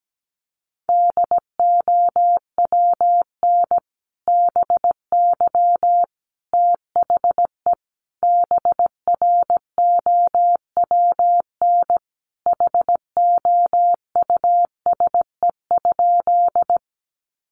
Morse code, heard as a keyed tone, about 17 wpm, 700 Hz. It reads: DOWN BY THE BROWN HOUSE?
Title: Sentences from Top 1200 Words 17wpm